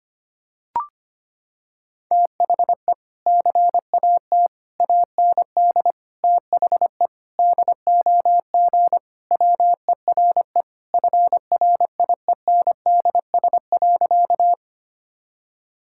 Morse code, heard as a keyed tone, about 25 wpm, 700 Hz. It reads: THE CAT AND THE DOG WERE FRIENDS.
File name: practice / 25wpm / Sentences with 2-Word Phrases - 25wpm